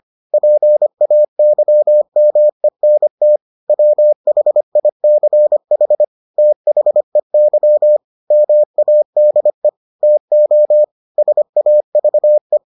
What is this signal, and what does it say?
Morse code, heard as a keyed tone, about 25 wpm, 600 Hz. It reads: PAYMENT WHICH THEY MADE TO SAVE